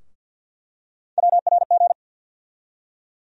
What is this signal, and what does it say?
Morse code, heard as a keyed tone, about 47 wpm, 700 Hz. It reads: WRG
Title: USA Call Sign Postfixes - 3 Letters 50wpm